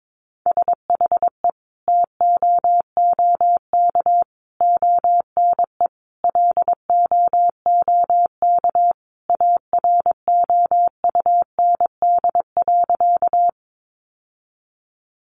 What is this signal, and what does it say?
Morse code, heard as a keyed tone, about 22 wpm, 700 Hz. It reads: SHE TOOK ONE LOOK AROUND.
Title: Sentences from Top 600 Words 22wpm